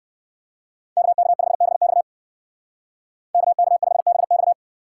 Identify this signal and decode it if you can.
Morse code, heard as a keyed tone, about 55 wpm, 700 Hz. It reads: KK5BX KK5BX
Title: Speed Racing - USA Call Signs 45wpm